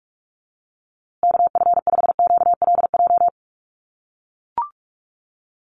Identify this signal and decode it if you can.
Morse code, heard as a keyed tone, about 44 wpm, 700 Hz. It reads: KF5QLJ
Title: USA Call Signs Repeated 3x 45wpm